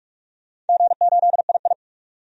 Morse code, heard as a keyed tone, about 44 wpm, 700 Hz. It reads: G8II
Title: DXCC-100 Call Signs - 4 Characters - 45wpm